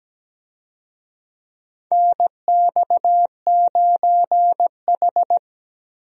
Morse code, heard as a keyed tone, about 17 wpm, 700 Hz. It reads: NX9H
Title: USA Call Signs - 4 Characters - Repeated 3x - 17wpm